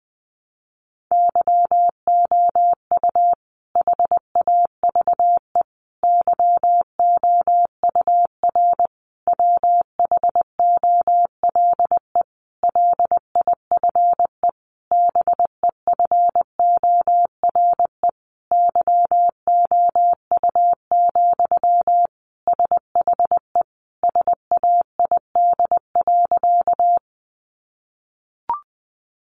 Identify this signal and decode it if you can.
Morse code, heard as a keyed tone, about 20 wpm, 700 Hz. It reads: YOU HAVE YOUR WHOLE LIFE BEFORE YOU, SHE SAID.
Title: Sentences from Top 400 Words 20wpm